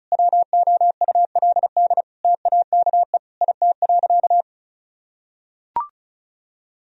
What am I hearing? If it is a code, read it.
Morse code, heard as a keyed tone, about 34 wpm, 700 Hz. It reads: WOULD TAKE IT.
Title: Sentences from Top 1000 Words - Review Part 4 - 35wpm